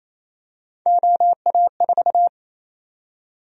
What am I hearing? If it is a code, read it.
Morse code, heard as a keyed tone, about 28 wpm, 700 Hz. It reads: OA4
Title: USA Call Sign Prefixes - Repeated 2x - 28wpm